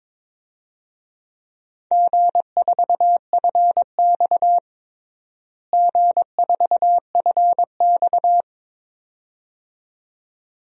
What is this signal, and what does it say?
Morse code, heard as a keyed tone, about 21 wpm, 700 Hz. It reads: G4FX G4FX
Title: DXCC-100 Call Signs - 4 Characters - Repeated 2x - 22wpm